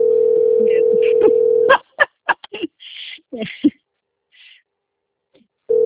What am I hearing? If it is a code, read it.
Ringback tone (North America), 440 + 480 Hz.